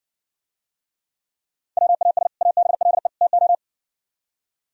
Morse code, heard as a keyed tone, about 58 wpm, 700 Hz. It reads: WAS ABLE TO